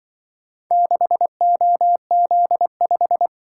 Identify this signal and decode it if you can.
Morse code, heard as a keyed tone, about 24 wpm, 700 Hz. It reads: 6OZ5
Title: Speed-Racing - Four Letter-Number Combinations - 20wpm